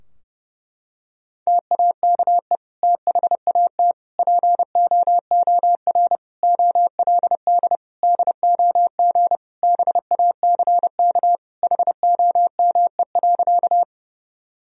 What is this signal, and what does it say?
Morse code, heard as a keyed tone, about 30 wpm, 700 Hz. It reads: TAKE THAT POOR OLD DOG BACK HOME.
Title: Sentences from Top 1000 Words - Review Part 2 - 30wpm